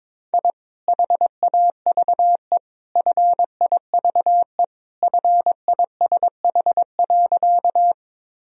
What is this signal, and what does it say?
Morse code, heard as a keyed tone, about 22 wpm, 700 Hz. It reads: I HAVE FIVE FISH.
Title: Sentences from Top 1000 Words - Review Part 2 - 22wpm